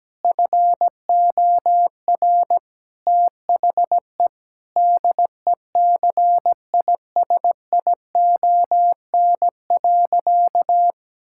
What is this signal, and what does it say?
Morse code, heard as a keyed tone, about 17 wpm, 700 Hz. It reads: FOR THE DECISION.
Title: Sentences from Top 1600 Words 17wpm